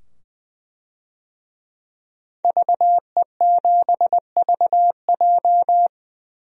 Morse code, heard as a keyed tone, about 20 wpm, 700 Hz. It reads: VE7VJ
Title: DXCC-100 Call Signs - 5 Characters - 20wpm